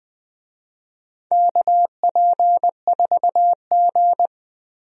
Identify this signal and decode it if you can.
Morse code, heard as a keyed tone, about 20 wpm, 700 Hz. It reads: KP4G
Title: DXCC-100 Call Signs - 4 Characters - 20wpm